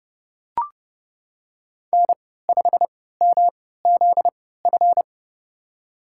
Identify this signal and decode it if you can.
Morse code, heard as a keyed tone, about 30 wpm, spaced as for 17 wpm, 700 Hz. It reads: N5MZF
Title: USA Call Signs - 5 Characters - 3x Character Spacing - 30wpm